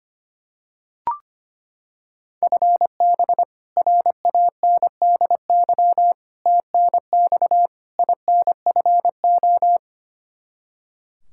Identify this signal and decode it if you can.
Morse code, heard as a keyed tone, about 25 wpm, 700 Hz. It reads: FB RANDY TNX INFO
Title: Practice QSO - Exchange 3 - More Info - 25wpm